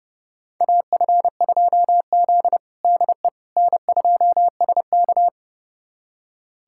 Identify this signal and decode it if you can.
Morse code, heard as a keyed tone, about 30 wpm, 700 Hz. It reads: AF2Z DE N2SK
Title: USA Callsign DE Callsign - 4 Characters - 30wpm